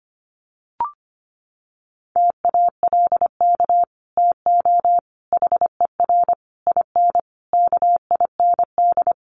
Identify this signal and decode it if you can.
Morse code, heard as a keyed tone, about 25 wpm, 700 Hz. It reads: TALK TO HER IN KIND
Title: Sentences from Top 1500 Words 25wpm